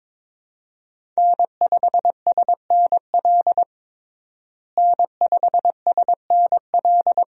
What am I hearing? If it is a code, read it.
Morse code, heard as a keyed tone, about 22 wpm, 700 Hz. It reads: N5SNL N5SNL
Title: USA Call Signs - 5 Characters - Repeated 2x - 22wpm